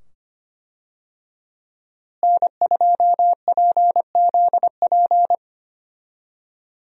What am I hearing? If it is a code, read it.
Morse code, heard as a keyed tone, about 25 wpm, 700 Hz. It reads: N2PZP
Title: USA Call Signs - 5 Characters - Repeated 2x - 25wpm